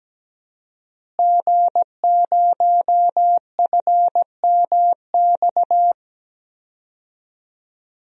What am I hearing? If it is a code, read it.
Morse code, heard as a keyed tone, about 17 wpm, 700 Hz. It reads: G0FMX
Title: DXCC-100 Call Signs - 5 Characters - Repeated 2x - 17wpm